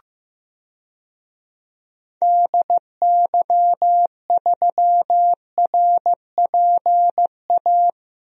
Morse code, heard as a keyed tone, about 16 wpm, 700 Hz. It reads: DY3RPA
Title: DXCC-100 Call Signs - 6 Characters - Repeated 2x - 15wpm